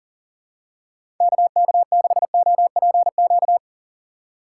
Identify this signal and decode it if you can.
Morse code, heard as a keyed tone, about 40 wpm, 700 Hz. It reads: KK6OPQ
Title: USA Call Signs - 6x Character Spacing - 40wpm